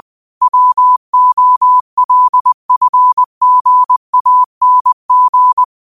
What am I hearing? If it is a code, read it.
Morse code, heard as a keyed tone, about 20 wpm, 1000 Hz. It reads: WOLFGANG
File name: qotdfiles / 20wpm